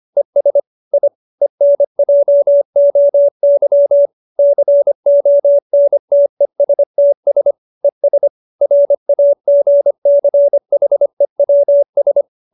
Morse code, heard as a keyed tone, about 25 wpm, 570 Hz. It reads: ES I ENJOY CONTESTS ES RAGCHEWS
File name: QSO315 25